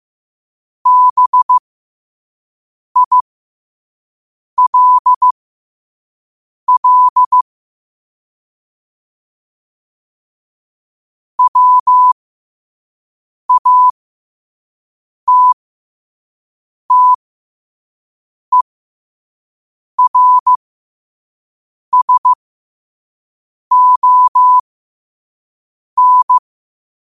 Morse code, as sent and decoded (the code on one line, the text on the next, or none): -... .. .-.. .-.. / .-- .- - - . .-. ... --- -.
BILL WATTERSON